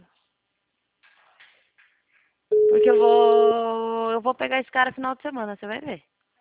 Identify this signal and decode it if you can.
Ringback tone (Europe, 425 Hz), with speech.